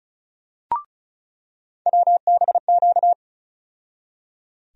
Morse code, heard as a keyed tone, about 34 wpm, 700 Hz. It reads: WBQ
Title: USA Call Sign Postfixes - 3 Letters 35wpm